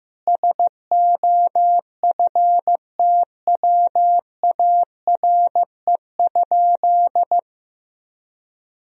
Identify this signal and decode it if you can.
Morse code, heard as a keyed tone, about 15 wpm, 700 Hz. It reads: SOFTWARE?
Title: Sentences from Top 1000 Words - Review Part 3 - 15wpm